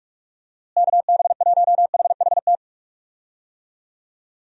Morse code, heard as a keyed tone, about 46 wpm, 700 Hz. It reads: KB1HHT
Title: USA Call Signs - 6 Characters - Repeated 3x - 45wpm